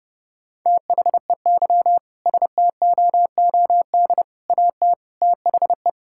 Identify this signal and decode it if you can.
Morse code, heard as a keyed tone, about 30 wpm, 700 Hz. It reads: THEY STOOD AT THE
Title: Sentences from Top 1600 Words 30wpm